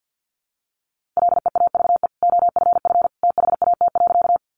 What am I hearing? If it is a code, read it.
Morse code, heard as a keyed tone, about 50 wpm, 700 Hz. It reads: LEAVE OFF THAT.
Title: Sentences from Top 400 Words 50wpm